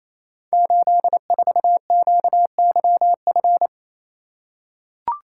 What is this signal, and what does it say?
Morse code, heard as a keyed tone, about 27 wpm, 700 Hz. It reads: 84QYF
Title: Code Groups - 5 Letter:Numbers - 28wpm